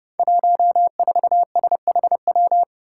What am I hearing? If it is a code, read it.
Morse code, heard as a keyed tone, about 29 wpm, 700 Hz. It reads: 14SHW
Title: Five Challenging Letter-Number Combinations 30wpm